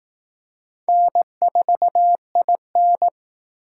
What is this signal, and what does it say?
Morse code, heard as a keyed tone, about 19 wpm, 700 Hz. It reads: N4IN
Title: Speed Racing - USA 1x2 Call Signs 15wpm